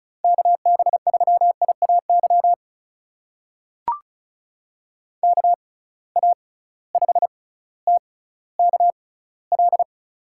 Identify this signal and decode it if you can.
Morse code, heard as a keyed tone, about 36 wpm, spaced as for 18 wpm, 690 Hz. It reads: KB3IAY KA5TKL
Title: USA Call Signs - 6 Characters - 6x Character Spacing - 35wpm